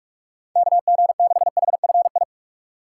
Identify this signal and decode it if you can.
Morse code, heard as a keyed tone, about 46 wpm, 700 Hz. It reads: KG6HFI
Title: USA Call Signs - 6 Characters - 3x Character Spacing - 45wpm